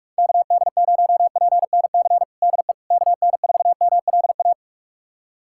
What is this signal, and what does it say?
Morse code, heard as a keyed tone, about 45 wpm, 700 Hz. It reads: KD0PNC DE KN4MLA
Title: USA Callsign DE Callsign - 6 Characters - 45wpm